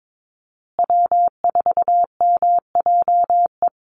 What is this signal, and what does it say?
Morse code, heard as a keyed tone, about 22 wpm, 700 Hz. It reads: W4MJE
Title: Call Signs Repeated 2x - 22wpm